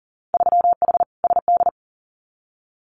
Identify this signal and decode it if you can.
Morse code, heard as a keyed tone, about 41 wpm, 700 Hz. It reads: 3H SD
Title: Winter Field Day Exchanges Encore 40wpm